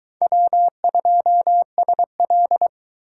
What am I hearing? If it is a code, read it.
Morse code, heard as a keyed tone, about 22 wpm, 700 Hz. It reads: W2SL